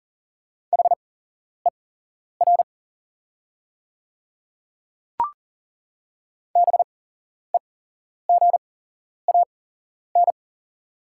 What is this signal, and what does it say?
Morse code, heard as a keyed tone, about 40 wpm, spaced as for 11 wpm, 690 Hz. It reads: HER BEGAN
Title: 8x Character Spacing - Top 500 Words - 40wpm